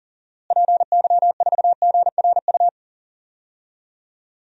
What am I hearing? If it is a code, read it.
Morse code, heard as a keyed tone, about 40 wpm, 700 Hz. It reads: PY4GRU
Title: DXCC-100 Call Signs - 6 Characters - Repeated 2x - 40wpm